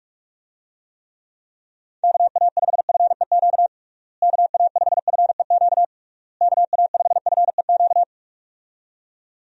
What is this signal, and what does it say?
Morse code, heard as a keyed tone, about 46 wpm, 700 Hz. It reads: KA5FEQ KA5FEQ KA5FEQ
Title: USA Call Signs - 6 Characters - Repeated 3x - 45wpm